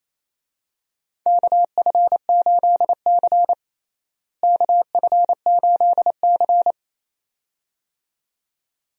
Morse code, heard as a keyed tone, about 28 wpm, 700 Hz. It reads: KF8C KF8C
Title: USA Call Signs - 4 Characters - Repeated 2x - 28wpm